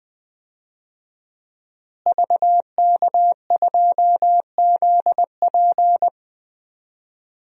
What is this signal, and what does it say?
Morse code, heard as a keyed tone, about 20 wpm, 700 Hz. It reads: VK2ZP
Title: DXCC-100 Call Signs - 5 Characters - Repeated 2x - 20wpm